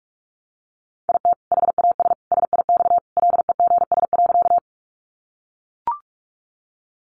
Morse code, heard as a keyed tone, about 44 wpm, 700 Hz. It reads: IT HAS SIX LEGS.
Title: Additional Easy Phrases 45wpm